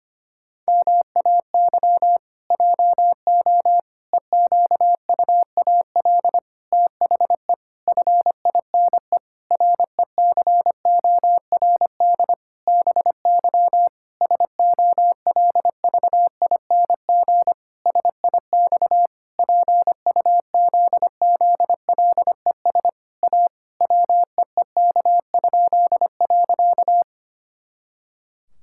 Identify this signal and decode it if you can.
Morse code, heard as a keyed tone, about 25 wpm, 700 Hz. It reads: MAY JO EQUAL THE FINE RECORD BY SOLVING SIX PUZZLES A WEEK?.
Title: Pangram Challenge v2 - 25wpm